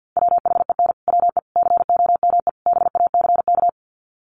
Morse code, heard as a keyed tone, about 50 wpm, 700 Hz. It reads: WHEN WE COME BACK
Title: Sets of 4 Words 50wpm